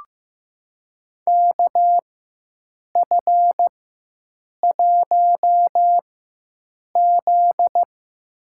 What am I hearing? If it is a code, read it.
Morse code, heard as a keyed tone, about 15 wpm, spaced as for 7 wpm, 700 Hz. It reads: KF1Z
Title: USA Call Signs - 4 Characters - 4x Character Spacing - 15wpm